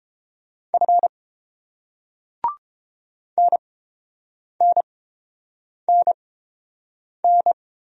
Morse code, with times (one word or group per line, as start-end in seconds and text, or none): 0.7-1.1 F
3.4-7.5 NNNN